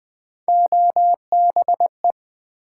Morse code, heard as a keyed tone, about 20 wpm, 700 Hz. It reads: OBE
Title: Build A Word - 1 Letter At A Time - 4-Letter Words - 20wpm